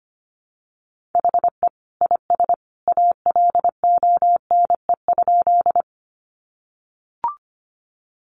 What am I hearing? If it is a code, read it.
Morse code, heard as a keyed tone, about 25 wpm, 700 Hz. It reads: HE IS ALONE?
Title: Sentences from Top 1000 Words 25wpm